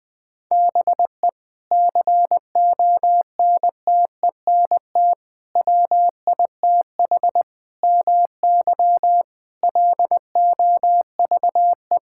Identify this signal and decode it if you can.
Morse code, heard as a keyed tone, about 20 wpm, 700 Hz. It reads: BE CONTENT WITH MY LOVE
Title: Sentences from Top 1000 Words - Review Part 6 - 20wpm